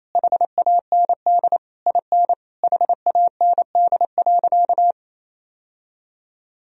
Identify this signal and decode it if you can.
Morse code, heard as a keyed tone, about 27 wpm, 700 Hz. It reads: HAND IN HAND.
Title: Sentences from Top 1000 Words - Review Part 3 - 28wpm